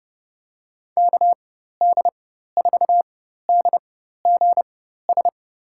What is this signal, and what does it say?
Morse code, heard as a keyed tone, about 30 wpm, spaced as for 14 wpm, 700 Hz. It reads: KD4DGS
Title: USA Call Signs - 4x Character Spacing - 30wpm